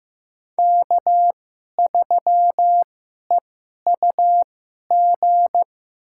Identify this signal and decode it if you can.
Morse code, heard as a keyed tone, about 15 wpm, spaced as for 11 wpm, 700 Hz. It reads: K3EUG
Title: USA Call Signs - 5 Characters - 2x Character Spacing - 15wpm